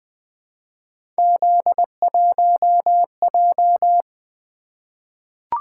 Morse code, heard as a keyed tone, about 20 wpm, 700 Hz. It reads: Z1J
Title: Three Letter-Number Combinations - 20wpm